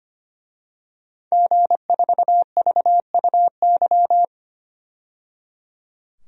Morse code, heard as a keyed tone, about 25 wpm, 700 Hz. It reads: G4VUY
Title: DXCC-100 Call Signs - 5 Characters - 25wpm